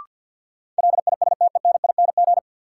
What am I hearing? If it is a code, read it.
Morse code, heard as a keyed tone, about 52 wpm, 700 Hz. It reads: LISTENING
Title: Sets of 5 Words - New Word Drill - 50wpm